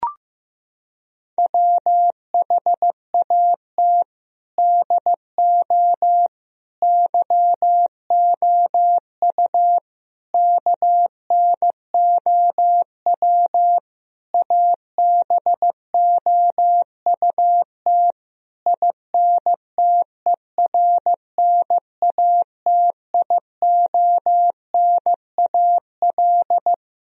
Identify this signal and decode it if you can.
Morse code, heard as a keyed tone, about 15 wpm, 700 Hz. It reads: WHAT DO YOU KNOW ABOUT INTERNATIONAL
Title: Sentences from Top 1000 Words - Review Part 4 - 15wpm